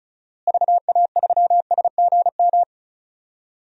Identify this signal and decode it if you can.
Morse code, heard as a keyed tone, about 36 wpm, 700 Hz. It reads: VA3SGM
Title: DXCC-100 Call Signs - 6 Characters - Repeated 2x - 35wpm